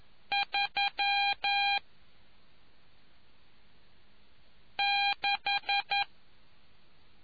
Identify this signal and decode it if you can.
Morse code, heard as a keyed tone, about 11 wpm, 800 Hz. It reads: 3 6